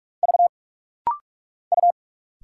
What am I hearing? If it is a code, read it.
Morse code, heard as a keyed tone, about 44 wpm, 700 Hz. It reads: V U
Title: Single Letter-Number Alt - Rapid-Fire 45wpm